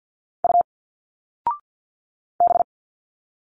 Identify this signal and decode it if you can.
Morse code, heard as a keyed tone, about 51 wpm, 700 Hz. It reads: U B
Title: Single Letters 50wpm